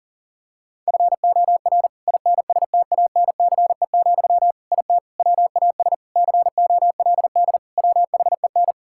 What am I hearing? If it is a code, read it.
Morse code, heard as a keyed tone, about 37 wpm, 700 Hz. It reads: FOR INSTANCE, IT WAS COLD WHEN
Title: Sentences from Top 1000 Words - Review Part 5 - 40wpm